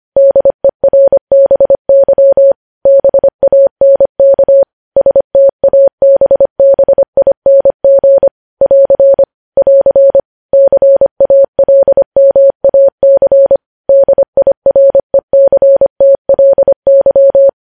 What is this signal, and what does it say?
Morse code, heard as a keyed tone, about 25 wpm, 550 Hz. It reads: DERBY BANK STABBING <AR> <AR> CALMAC DIRECTLY
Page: news-headlines-in-morse-code-at-25-wpm